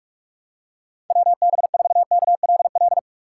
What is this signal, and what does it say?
Morse code, heard as a keyed tone, about 44 wpm, 700 Hz. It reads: WB4KLL
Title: USA Call Signs - 6 Characters - 6x Character Spacing - 45wpm